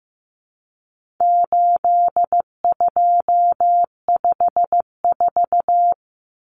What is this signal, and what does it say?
Morse code, heard as a keyed tone, about 15 wpm, 700 Hz. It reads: 8254
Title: Common Cut Numbers - 4-Digit 15wpm